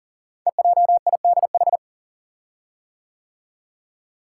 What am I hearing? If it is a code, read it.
Morse code, heard as a keyed tone, about 39 wpm, 700 Hz. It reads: EJIDH
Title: Code Groups - 5 Letters - 40wpm